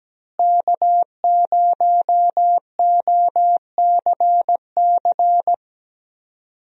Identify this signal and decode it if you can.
Morse code, heard as a keyed tone, about 17 wpm, 700 Hz. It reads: K0OCC
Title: USA Call Signs - 5 Characters - Repeated 2x - 17wpm